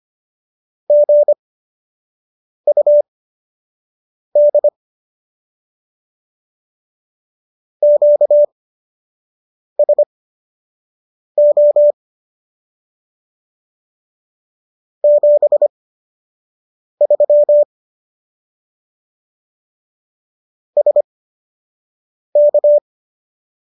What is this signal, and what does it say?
Morse code, heard as a keyed tone, about 25 wpm, spaced as for 6 wpm, 590 Hz. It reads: GUD QSO 73 SK